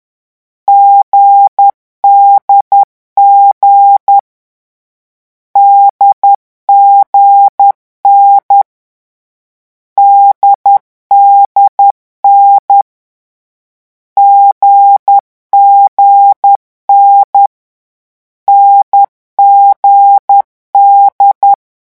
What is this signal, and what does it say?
Morse code, heard as a keyed tone, about 10 wpm, 800 Hz. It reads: GDG DGN DDN GGN NGD
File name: morse mp3 / lecon4-vitesse 10